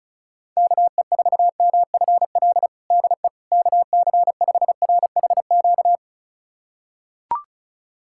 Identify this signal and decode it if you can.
Morse code, heard as a keyed tone, about 36 wpm, 690 Hz. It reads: KE4MFL DE KC5RHQ
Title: USA Callsign DE Callsign 35wpm